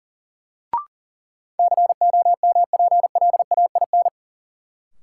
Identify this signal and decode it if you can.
Morse code, heard as a keyed tone, about 40 wpm, 700 Hz. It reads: COMPLAIN
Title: Sentences from Top 1300 Words - New Word Drill Alt - 40wpm